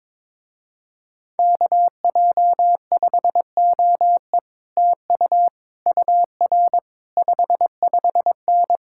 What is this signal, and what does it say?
Morse code, heard as a keyed tone, about 22 wpm, 700 Hz. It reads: KJ5OE TU UR 55N
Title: Speed-Racing Common QSO Phrases Encore - 22wpm